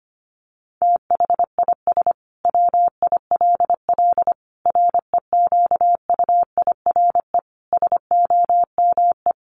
Morse code, heard as a keyed tone, about 25 wpm, 700 Hz. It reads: THIS WILL REQUIRE SOME
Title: Sentences from Top 900 Words 25wpm